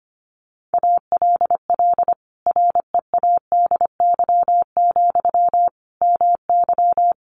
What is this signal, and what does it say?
Morse code, heard as a keyed tone, about 25 wpm, 700 Hz. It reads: ALL READY, MY
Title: Sentences from Top 1300 Words 25wpm